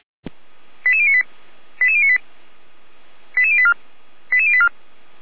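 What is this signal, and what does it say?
Five-tone selective call (ZVEI), heard as quick runs of five tones, tones of about 70 ms. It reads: 88098 88098 88084 88084
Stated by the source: ZVEI five-tone selective call received on an SDR.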